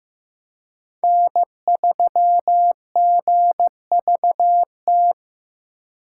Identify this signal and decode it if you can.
Morse code, heard as a keyed tone, about 15 wpm, 700 Hz. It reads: N3GVT
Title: USA Call Signs - 5 Characters - 3x Character Spacing - 15wpm